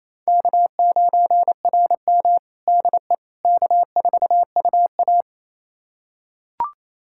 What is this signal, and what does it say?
Morse code, heard as a keyed tone, about 27 wpm, 690 Hz. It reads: K9RM DE K4UA
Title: USA Callsign DE Callsign - 4 Characters - 28wpm